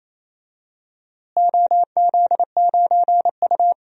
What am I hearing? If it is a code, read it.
Morse code, heard as a keyed tone, about 28 wpm, 700 Hz. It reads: OZ9U
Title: DXCC-100 Call Signs - 4 Characters - Repeated 3x - 28wpm